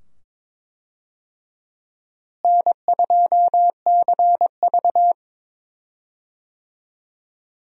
Morse code, heard as a keyed tone, about 22 wpm, 700 Hz. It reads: N2CV
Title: USA Call Signs - 4 Characters - Repeated 2x - 22wpm